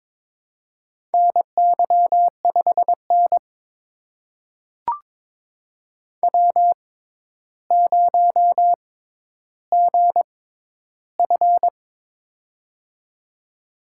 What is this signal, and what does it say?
Morse code, heard as a keyed tone, about 22 wpm, spaced as for 11 wpm, 700 Hz. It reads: NY5N W0GF
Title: USA Call Signs - 4 Characters - 6x Character Spacing - 22wpm